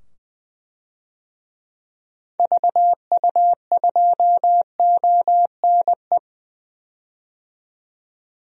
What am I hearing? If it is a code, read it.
Morse code, heard as a keyed tone, about 20 wpm, 700 Hz. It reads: VU2ONE
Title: DXCC-100 Call Signs - 6 Characters - 20wpm